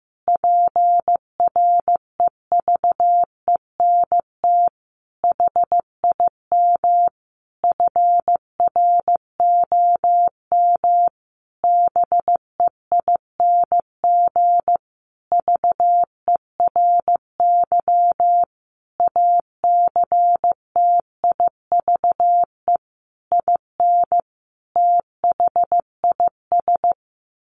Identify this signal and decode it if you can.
Morse code, heard as a keyed tone, about 15 wpm, 700 Hz. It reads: PREVENT HIM FROM BEING VERY ACTIVE IN THIS
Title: Sentences from Top 1500 Words 15wpm